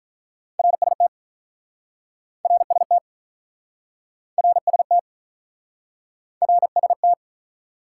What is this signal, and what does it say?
Morse code, heard as a keyed tone, about 47 wpm, 700 Hz. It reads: RST RST RST RST